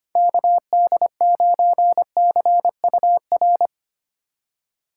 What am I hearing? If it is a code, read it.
Morse code, heard as a keyed tone, about 32 wpm, 700 Hz. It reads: KD9CUR